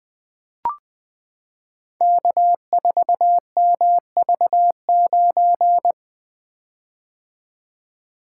Morse code, heard as a keyed tone, about 20 wpm, 700 Hz. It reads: K4MV9
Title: Code Groups - 5 Letter:Numbers - 20wpm